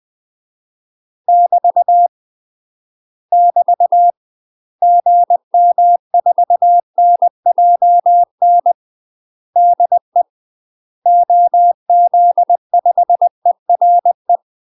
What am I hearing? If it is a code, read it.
Morse code, heard as a keyed tone, about 20 wpm, 700 Hz. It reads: <BT> <BT> GM4NJN DE OZ5ERE
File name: UK MASTER QSOs TRAINING / 20WPM / UK MASTER QSOs 0010